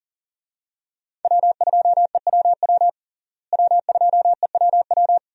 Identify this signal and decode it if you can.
Morse code, heard as a keyed tone, about 40 wpm, 700 Hz. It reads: W2EWW W2EWW
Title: USA Call Signs - 5 Characters - Repeated 2x - 40wpm